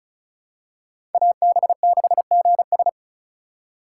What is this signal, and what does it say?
Morse code, heard as a keyed tone, about 34 wpm, 700 Hz. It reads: AB6GS
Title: USA Call Signs - 5 Characters - 4x Character Spacing - 35wpm